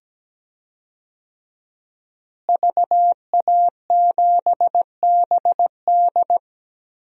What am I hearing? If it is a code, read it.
Morse code, heard as a keyed tone, about 17 wpm, 700 Hz. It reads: VA7BD
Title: DXCC-100 Call Signs - 5 Characters - Repeated 2x - 17wpm